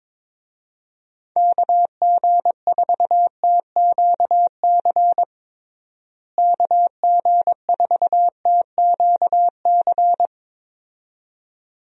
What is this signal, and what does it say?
Morse code, heard as a keyed tone, about 22 wpm, 700 Hz. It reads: KG4TQC KG4TQC
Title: USA Call Signs - 6 Characters - Repeated 2x - 22wpm